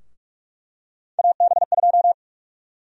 Morse code, heard as a keyed tone, about 44 wpm, 690 Hz. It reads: AB2
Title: USA Call Sign Prefixes - Repeated 2x - 45wpm